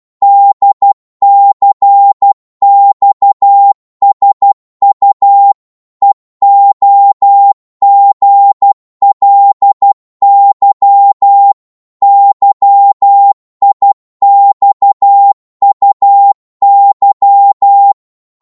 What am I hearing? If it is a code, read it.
Morse code, heard as a keyed tone, about 12 wpm, 800 Hz. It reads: DCXSU EOGLY YIXUY
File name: Morse / ltr-12wpm